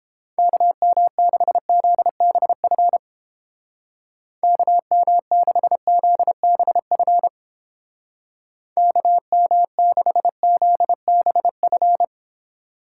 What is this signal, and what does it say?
Morse code, heard as a keyed tone, about 29 wpm, 700 Hz. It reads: KM6ZBF KM6ZBF KM6ZBF